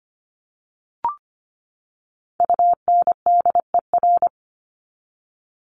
Morse code, heard as a keyed tone, about 25 wpm, 700 Hz. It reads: UNDER
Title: Top 300 Words 25wpm